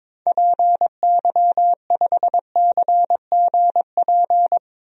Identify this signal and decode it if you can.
Morse code, heard as a keyed tone, about 22 wpm, 700 Hz. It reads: PY5CGP
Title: DXCC-100 Call Signs - 6 Characters - Repeated 3x - 22wpm